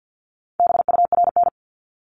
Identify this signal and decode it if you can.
Morse code, heard as a keyed tone, about 50 wpm, 700 Hz. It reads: BURN
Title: Multi-letter Phonemes Repeated 3x 50wpm